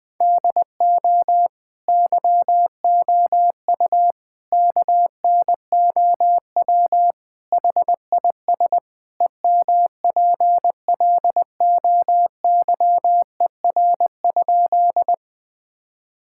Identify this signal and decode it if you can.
Morse code, heard as a keyed tone, about 20 wpm, 700 Hz. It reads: DO YOU KNOW HIS EMPLOYER?
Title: Sentences from Top 1600 Words 20wpm